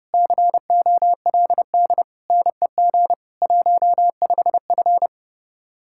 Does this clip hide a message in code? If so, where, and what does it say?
Yes: Morse code from 0.1 to 5.1 s, heard as a keyed tone, about 39 wpm, 700 Hz. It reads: COLD NEG 15F